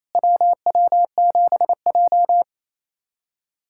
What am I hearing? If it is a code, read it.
Morse code, heard as a keyed tone, about 28 wpm, 690 Hz. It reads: WW7J
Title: SOTA QSO Practice 28wpm